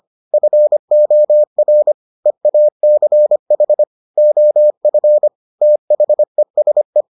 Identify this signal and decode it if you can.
Morse code, heard as a keyed tone, about 25 wpm, 600 Hz. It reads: FOR EACH OF THESE